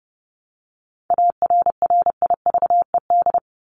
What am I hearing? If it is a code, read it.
Morse code, heard as a keyed tone, about 30 wpm, 700 Hz. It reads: ARRIVED
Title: ED Suffix - Top 100 Words with Base Words 30wpm